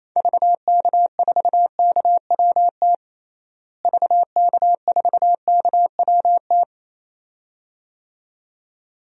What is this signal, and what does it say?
Morse code, heard as a keyed tone, about 28 wpm, 700 Hz. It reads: VK4KWT VK4KWT
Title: DXCC-100 Call Signs - 6 Characters - Repeated 2x - 28wpm